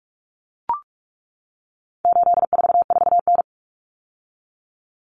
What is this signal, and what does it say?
Morse code, heard as a keyed tone, about 44 wpm, 700 Hz. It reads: 844N
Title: Common Cut Numbers - 4-Digit 45wpm